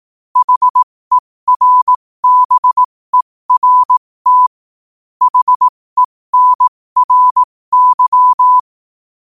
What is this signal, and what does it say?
Morse code, heard as a keyed tone, about 18 wpm, spaced as for 15 wpm, 1000 Hz. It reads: HERBERT HENRY
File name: qotdfiles / 15wpm